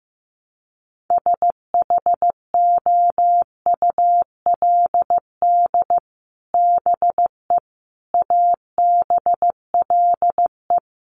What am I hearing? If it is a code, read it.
Morse code, heard as a keyed tone, about 15 wpm, 700 Hz. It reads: SHOULD BE ABLE
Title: Sets of 4 Words 15wpm